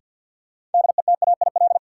Morse code, heard as a keyed tone, about 53 wpm, 700 Hz. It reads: DETAIL
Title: Sentences from Top 1500 Words - New Word Drill Alt - 50wpm